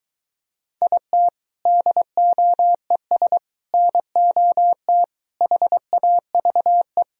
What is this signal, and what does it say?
Morse code, heard as a keyed tone, about 23 wpm, 700 Hz. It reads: IT DOES NOT HAVE